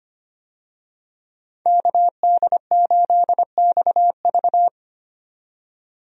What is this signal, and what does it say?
Morse code, heard as a keyed tone, about 25 wpm, 700 Hz. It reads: KD8XV
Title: USA Call Signs - 5 Characters - 25wpm